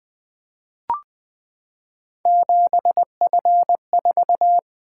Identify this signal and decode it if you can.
Morse code, heard as a keyed tone, about 20 wpm, 700 Hz. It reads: 7F4
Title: Three Letter-Number Combinations - 20wpm